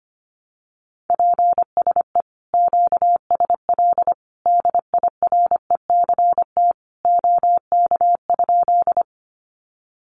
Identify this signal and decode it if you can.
Morse code, heard as a keyed tone, about 25 wpm, 700 Hz. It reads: PSE QSL DIRECT OK?
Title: Common QSO Phrases 25wpm